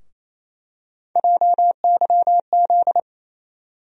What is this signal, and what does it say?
Morse code, heard as a keyed tone, about 28 wpm, 700 Hz. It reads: JYZ
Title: USA Call Sign Postfixes - 3 Letters 28wpm